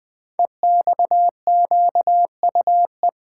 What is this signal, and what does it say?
Morse code, heard as a keyed tone, about 20 wpm, 700 Hz. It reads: EXQUE
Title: Code Groups - 5 Letter:Numbers - 20wpm